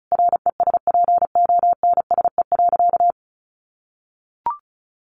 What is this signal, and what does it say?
Morse code, heard as a keyed tone, about 34 wpm, 700 Hz. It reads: RESPONSE.
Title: Sentences from Top 1400 Words 35wpm